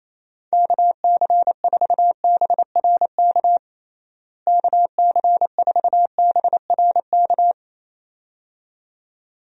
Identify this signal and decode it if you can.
Morse code, heard as a keyed tone, about 27 wpm, 700 Hz. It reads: KC4BRK KC4BRK
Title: USA Call Signs - 6 Characters - Repeated 2x - 28wpm